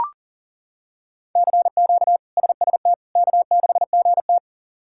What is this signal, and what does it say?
Morse code, heard as a keyed tone, about 40 wpm, 700 Hz. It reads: CQ SST K6GT
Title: K1USN SST QSO Practice - 40wpm